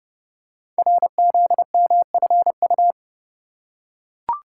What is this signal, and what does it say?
Morse code, heard as a keyed tone, about 30 wpm, 700 Hz. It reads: RZMFU
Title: Code Groups - 5 Letter:Numbers - 30wpm